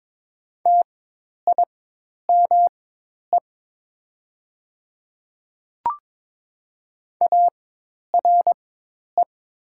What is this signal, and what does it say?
Morse code, heard as a keyed tone, about 23 wpm, spaced as for 10 wpm, 700 Hz. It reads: TIME ARE
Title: 4x Character Spacing - Top 300 Words - 22wpm